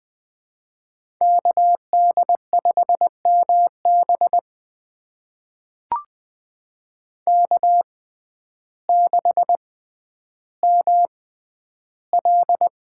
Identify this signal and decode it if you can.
Morse code, heard as a keyed tone, about 20 wpm, spaced as for 11 wpm, 700 Hz. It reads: KD5MB K6ML
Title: USA Call Signs - 5 Characters - 6x Character Spacing - 20wpm